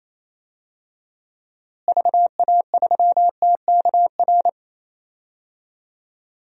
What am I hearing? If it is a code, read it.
Morse code, heard as a keyed tone, about 27 wpm, 700 Hz. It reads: VA3TKR
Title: DXCC-100 Call Signs - 6 Characters - Repeated 2x - 28wpm